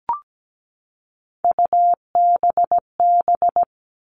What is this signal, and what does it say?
Morse code, heard as a keyed tone, about 17 wpm, 700 Hz. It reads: UBB
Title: All Cut Numbers - 3-Digit 17wpm